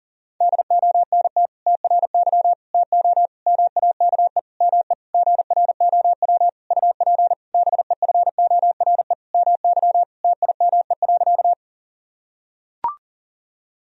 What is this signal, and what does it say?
Morse code, heard as a keyed tone, about 40 wpm, 700 Hz. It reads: DONT TRY TO MAKE ME GROW UP BEFORE MY TIME.
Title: Sentences from Top 1000 Words - Review Part 1 - 40wpm